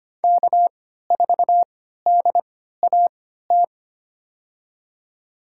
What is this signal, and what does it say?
Morse code, heard as a keyed tone, about 25 wpm, spaced as for 14 wpm, 700 Hz. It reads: K4DAT
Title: USA Call Signs - 5 Characters - 3x Character Spacing - 25wpm